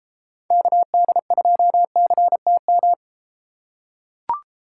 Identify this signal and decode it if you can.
Morse code, heard as a keyed tone, about 33 wpm, 700 Hz. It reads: KD2CTM